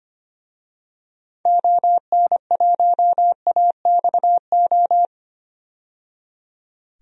Morse code, heard as a keyed tone, about 25 wpm, 700 Hz. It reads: ON1AXO
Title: DXCC-100 Call Signs - 6 Characters - 25wpm